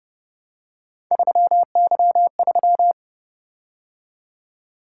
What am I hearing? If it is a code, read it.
Morse code, heard as a keyed tone, about 29 wpm, 700 Hz. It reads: 3Y3